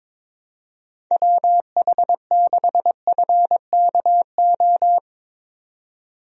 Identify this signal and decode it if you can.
Morse code, heard as a keyed tone, about 22 wpm, 700 Hz. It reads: WH6FKO
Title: USA Call Signs - 6 Characters - 22wpm